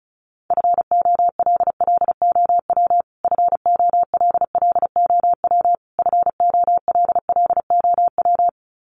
Morse code, heard as a keyed tone, about 34 wpm, 700 Hz. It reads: FOLLOW FOLLOW FOLLOW
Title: Top 500 Words Repeated 35wpm